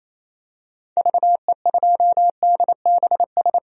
Morse code, heard as a keyed tone, about 28 wpm, 700 Hz. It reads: VE2DBS
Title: DXCC-100 Call Signs - 6 Characters - Repeated 2x - 28wpm